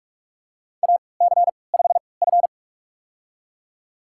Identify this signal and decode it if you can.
Morse code, heard as a keyed tone, about 46 wpm, spaced as for 25 wpm, 700 Hz. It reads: AC5F
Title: USA Call Signs - 4 Characters - 3x Character Spacing - 45wpm